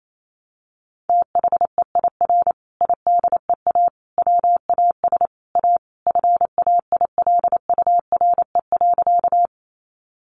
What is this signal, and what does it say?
Morse code, heard as a keyed tone, about 28 wpm, 700 Hz. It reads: THEIR IDEA WAS A FAILURE.
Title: Sentences from Top 900 Words 28wpm